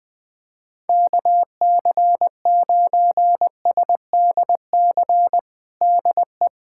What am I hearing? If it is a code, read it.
Morse code, heard as a keyed tone, about 21 wpm, 700 Hz. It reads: KC9SDC DE
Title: USA Callsign DE Callsign - 6 Characters - 20wpm